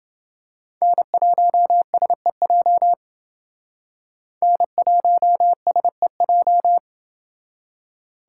Morse code, heard as a keyed tone, about 29 wpm, 700 Hz. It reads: N1SEJ N1SEJ